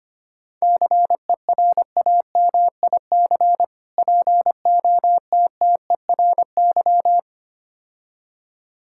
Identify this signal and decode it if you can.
Morse code, heard as a keyed tone, about 26 wpm, 700 Hz. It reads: CERAMIC POTTERY
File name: practice / 25wpm / Difficult Words - Speed-Racing with Hints - 25wpm